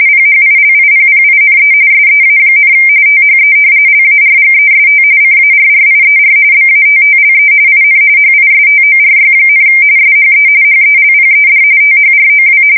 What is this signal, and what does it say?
RTTY, about 45.45 baud. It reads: PHILIPPINE SEA, THE LARGEST CARRIER BATTLE IN HISTORY, SHE HELPED THROW
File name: fsk170 / battleshipnj